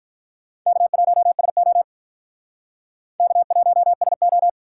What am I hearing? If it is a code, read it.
Morse code, heard as a keyed tone, about 44 wpm, 700 Hz. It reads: K1SO K1SO